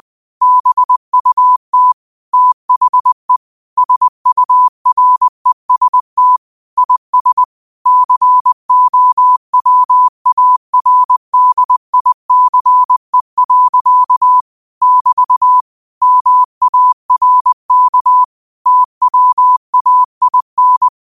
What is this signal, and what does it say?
Morse code, heard as a keyed tone, about 20 wpm, 1000 Hz. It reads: BUT THE SUREST IS COWARDICE. <BT> MARK TWAIN
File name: qotdfiles / 20wpm